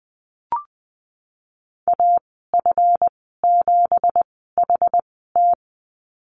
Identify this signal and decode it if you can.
Morse code, heard as a keyed tone, about 20 wpm, spaced as for 15 wpm, 700 Hz. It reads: AF7HT
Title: 231 Lesson 37g - Letter x - Call Signs 20wpm